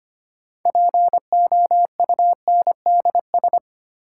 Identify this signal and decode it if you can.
Morse code, heard as a keyed tone, about 25 wpm, 700 Hz. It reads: POUNDS
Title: Sentences from Top 1000 Words - Review Part 2 - New Word Drill - 25wpm